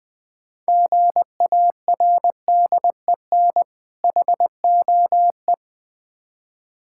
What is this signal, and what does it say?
Morse code, heard as a keyed tone, about 22 wpm, 690 Hz. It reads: GARDEN HOE
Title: Difficult Words - Speed-Racing with Hints - 20wpm